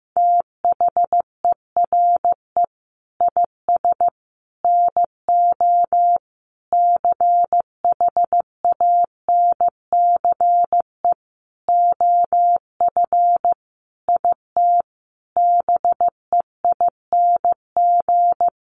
Morse code, heard as a keyed tone, about 15 wpm, 700 Hz. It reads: THERE IS NO CHANCE OF IT BEING
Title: Sentences from Top 800 Words 15wpm